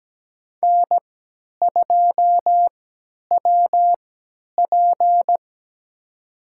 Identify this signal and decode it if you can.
Morse code, heard as a keyed tone, about 17 wpm, spaced as for 10 wpm, 700 Hz. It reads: N2WP
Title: USA Call Signs - 4 Characters - 3x Character Spacing - 17wpm